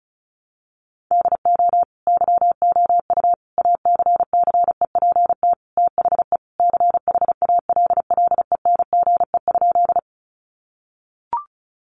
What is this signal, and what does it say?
Morse code, heard as a keyed tone, about 34 wpm, 700 Hz. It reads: DO YOU ACCEPT THE CHALLENGE?
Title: Sentences from Top 1300 Words 35wpm